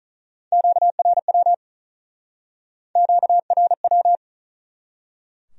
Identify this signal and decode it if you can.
Morse code, heard as a keyed tone, about 41 wpm, 700 Hz. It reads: QRW QRW
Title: Speed-Racing Q-Codes - 35wpm